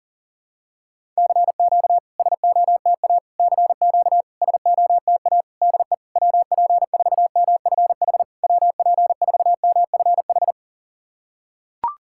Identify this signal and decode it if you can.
Morse code, heard as a keyed tone, about 40 wpm, 690 Hz. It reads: CQ SOTA CQ SOTA DE WP4MFH WP4MFH
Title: SOTA QSO Practice 40wpm